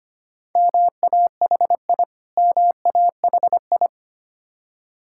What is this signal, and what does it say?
Morse code, heard as a keyed tone, about 25 wpm, 700 Hz. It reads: MAHI MAHI
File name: practice / 25wpm / Difficult Words with Hints - 25wpm